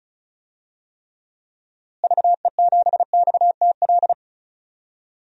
Morse code, heard as a keyed tone, about 34 wpm, 700 Hz. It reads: VE7XTL
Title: DXCC-100 Call Signs - 6 Characters - Repeated 3x - 35wpm